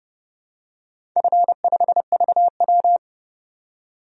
Morse code, heard as a keyed tone, about 30 wpm, 700 Hz. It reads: F5VW
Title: DXCC-100 Call Signs - 4 Characters - 30wpm